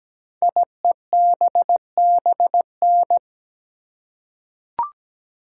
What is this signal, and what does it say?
Morse code, heard as a keyed tone, about 17 wpm, 700 Hz. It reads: IEBBN
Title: Code Groups - 5 Letters - 17wpm